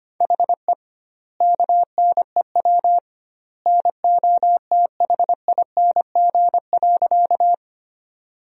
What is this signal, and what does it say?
Morse code, heard as a keyed tone, about 25 wpm, 700 Hz. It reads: HE KNEW NOTHING.
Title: 2x Word-Spacing - Sentences from Top 500 Words - 25wpm